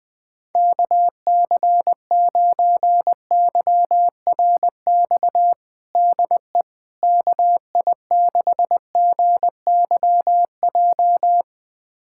Morse code, heard as a keyed tone, about 20 wpm, 700 Hz. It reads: KC9YRX DE KI6GYJ
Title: USA Callsign DE Callsign 20wpm